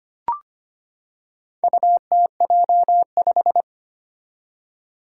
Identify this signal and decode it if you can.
Morse code, heard as a keyed tone, about 25 wpm, 700 Hz. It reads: UTJ5
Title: Four Challenging Letter-Number Combinations 25wpm